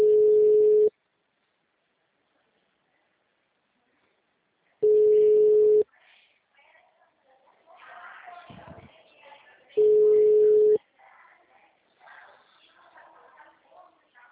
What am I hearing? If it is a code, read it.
Ringback tone (Europe, 425 Hz), with speech.
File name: out- -2002- - - .86